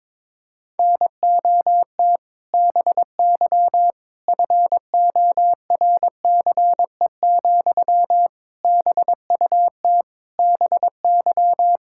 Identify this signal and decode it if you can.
Morse code, heard as a keyed tone, about 22 wpm, 700 Hz. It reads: NOT BY FORCE, BUT BY
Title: Sentences from Top 1000 Words - Review Part 5 - 22wpm